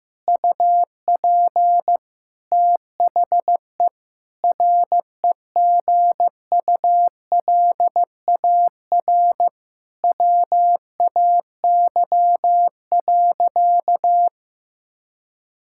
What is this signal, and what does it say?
Morse code, heard as a keyed tone, about 16 wpm, 700 Hz. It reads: UP THE REGULAR WAY.
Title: Sentences from Top 1000 Words - Review Part 6 - 15wpm